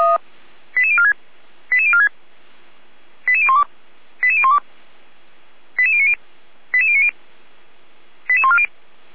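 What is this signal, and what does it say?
Five-tone selective call (ZVEI), heard as quick runs of five tones, tones of about 70 ms. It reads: 88046 88046 88012 88012 88090 88090 88140
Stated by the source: ZVEI five-tone call 88501 received by SDR, archived by fire brigade Feuerwehr Orlamünde.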